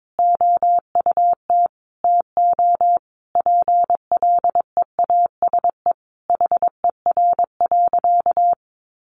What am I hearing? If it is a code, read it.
Morse code, heard as a keyed tone, about 22 wpm, 700 Hz. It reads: OUT TO PLEASE HER.
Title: Sentences from Top 1500 Words 22wpm